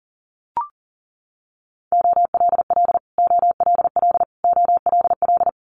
Morse code, heard as a keyed tone, about 38 wpm, 700 Hz. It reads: OLL OLL OLL
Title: Multi-letter Phonemes Repeated 3x 40wpm